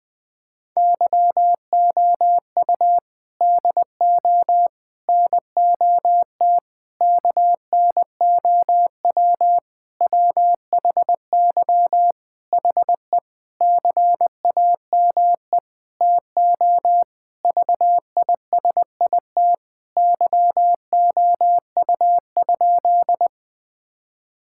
Morse code, heard as a keyed tone, about 20 wpm, 700 Hz. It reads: YOU DO NOT KNOW WHY HE CAME TO VISIT YOU?
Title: Sentences from Top 1000 Words - Review Part 6 - 20wpm